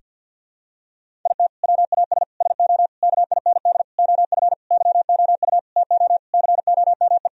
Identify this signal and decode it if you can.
Morse code, heard as a keyed tone, about 50 wpm, 700 Hz. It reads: IT WAS SO KIND OF YOU TO COME
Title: Sentences from Top 1000 Words - Review Part 3 - 50wpm